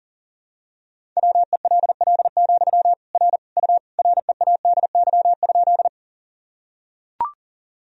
Morse code, heard as a keyed tone, about 40 wpm, 700 Hz. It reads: WELL, R U READY?
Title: Sentences from Top 400 Words and Abbreviations 40wpm